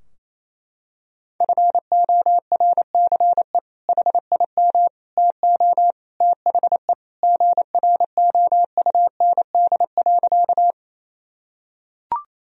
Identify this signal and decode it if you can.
Morse code, heard as a keyed tone, about 27 wpm, 700 Hz. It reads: FORCE HIM TO THE GROUND.
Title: Sentences from Top 1000 Words - Review Part 3 - 28wpm